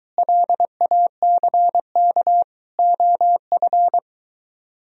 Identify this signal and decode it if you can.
Morse code, heard as a keyed tone, about 22 wpm, 700 Hz. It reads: LACK OF